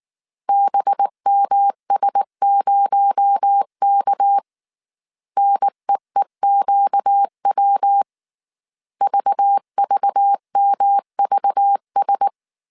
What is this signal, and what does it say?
Morse code, heard as a keyed tone, about 19 wpm, 800 Hz. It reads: BMS0K NEEQW VVMVS